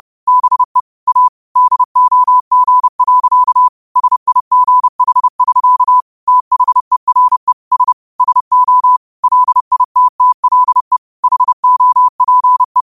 Morse code, heard as a keyed tone, about 30 wpm, 1000 Hz. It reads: BE A DOG. SIGH<SK> THERE S SO LITTLE HOPE